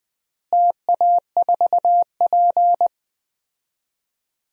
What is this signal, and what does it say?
Morse code, heard as a keyed tone, about 20 wpm, 700 Hz. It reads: TA4P
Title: DXCC-100 Call Signs - 4 Characters - Repeated 2x - 20wpm